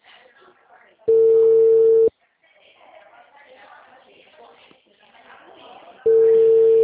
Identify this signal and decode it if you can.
Ringback tone (China, 450 Hz), with speech.